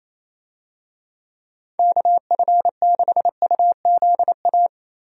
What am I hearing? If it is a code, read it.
Morse code, heard as a keyed tone, about 28 wpm, 700 Hz. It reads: KF6UZA
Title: USA Call Signs - 6 Characters - Repeated 3x - 28wpm